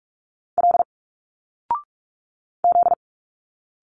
Morse code, heard as a keyed tone, about 46 wpm, 700 Hz. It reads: L Z
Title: Single Letters 45wpm